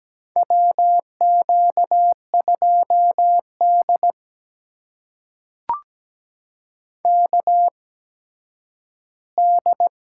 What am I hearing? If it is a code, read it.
Morse code, heard as a keyed tone, about 17 wpm, spaced as for 10 wpm, 700 Hz. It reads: WQ2D KD
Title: USA Call Signs - 4 Characters - 8x Character Spacing - 17wpm